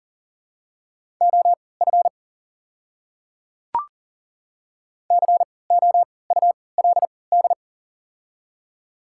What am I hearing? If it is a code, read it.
Morse code, heard as a keyed tone, about 37 wpm, spaced as for 22 wpm, 700 Hz. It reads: OF COULD
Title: 3x Character Spacing - Top 100 Words - 40wpm